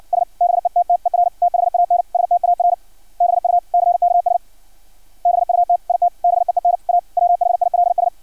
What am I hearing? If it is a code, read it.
Morse code, heard as a keyed tone, about 56 wpm, 700 Hz. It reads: S BETTER THAN STARBUCKS BUT IT BEEN A WHILS
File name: archived / CRSnet